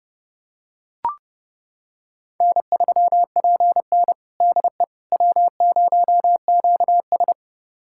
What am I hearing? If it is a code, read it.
Morse code, heard as a keyed tone, about 30 wpm, 700 Hz. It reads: N3PN DE W0QS
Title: USA Callsign DE Callsign - 4 Characters - 30wpm